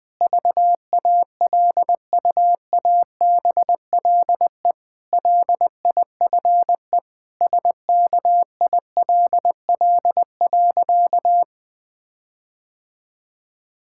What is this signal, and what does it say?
Morse code, heard as a keyed tone, about 20 wpm, 700 Hz. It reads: VALUABLE LIFE SKILL.
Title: Difficult Sentences with Hints - 20wpm